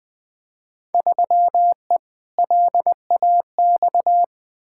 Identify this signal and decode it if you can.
Morse code, heard as a keyed tone, about 21 wpm, 700 Hz. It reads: 3E LAX
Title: Field Day Exchanges Encore - 20wpm